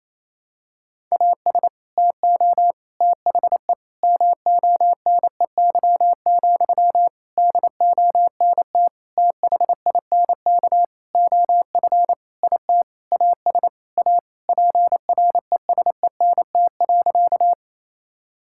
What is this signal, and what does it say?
Morse code, heard as a keyed tone, about 28 wpm, 700 Hz. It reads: AS TO THE MONEY, DONT THINK OF IT AS A PRESENT.
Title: Sentences from Top 1000 Words - Review Part 5 - 28wpm